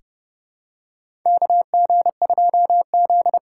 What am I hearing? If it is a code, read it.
Morse code, heard as a keyed tone, about 30 wpm, 700 Hz. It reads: KG2Z
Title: USA Call Signs - 4 Characters - 3x Character Spacing - 30wpm